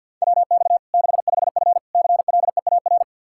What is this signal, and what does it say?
Morse code, heard as a keyed tone, about 53 wpm, 700 Hz. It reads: WX 65F CLEAR
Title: Practice QSO - Exchange 2 - Info - 50wpm